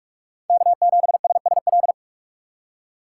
Morse code, heard as a keyed tone, about 43 wpm, 700 Hz. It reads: K7SSL
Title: USA Call Signs - 4x Character Spacing - 45wpm